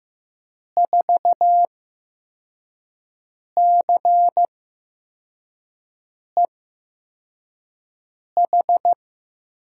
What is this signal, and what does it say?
Morse code, heard as a keyed tone, about 15 wpm, spaced as for 4 wpm, 700 Hz. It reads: 4CEH
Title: USA Call Signs - 6 Characters - 8x Character Spacing - 15wpm